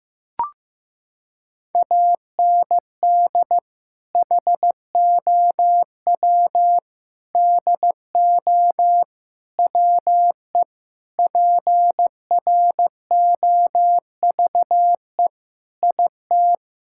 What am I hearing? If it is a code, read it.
Morse code, heard as a keyed tone, about 14 wpm, 700 Hz. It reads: AND HOW DO WE PROVE IT
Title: Sentences from Top 1600 Words 15wpm